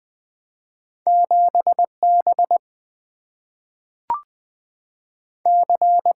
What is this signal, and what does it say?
Morse code, heard as a keyed tone, about 20 wpm, 700 Hz. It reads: 7B C
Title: Single-Letter Single-Number 20wpm